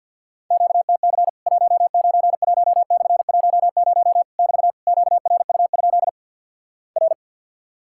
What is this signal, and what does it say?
Morse code, heard as a keyed tone, about 50 wpm, 700 Hz. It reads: QTC 191/10 <BT> QRU? R
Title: Worked All Europe DX Contest - QTC Exchange with Repeats - 50wpm